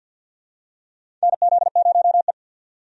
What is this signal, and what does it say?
Morse code, heard as a keyed tone, about 50 wpm, 700 Hz. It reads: NZ0E
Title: USA Call Signs - 4 Characters - 50wpm